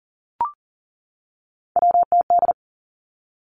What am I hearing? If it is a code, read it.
Morse code, heard as a keyed tone, about 42 wpm, 700 Hz. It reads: WTD
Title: Three Challenging Letter-Number Combinations 40wpm